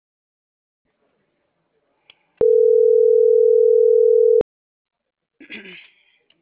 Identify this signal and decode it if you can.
Ringback tone (North America), 440 + 480 Hz.